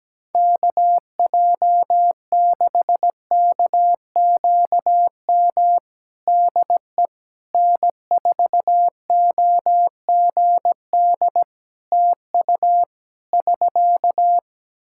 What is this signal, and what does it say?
Morse code, heard as a keyed tone, about 17 wpm, 700 Hz. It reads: KJ6KQM DE N4OGD TU <SK>
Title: Practice QSO - Exchange 4 - Ending - 17wpm